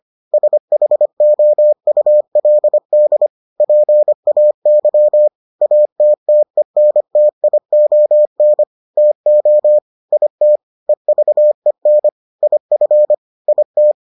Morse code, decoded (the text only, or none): SHOULD PAY ATTENTION TO IT EVEN IF IT